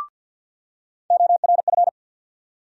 Morse code, heard as a keyed tone, about 47 wpm, 700 Hz. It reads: ORF
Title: USA Call Sign Postfixes - 3 Letters 50wpm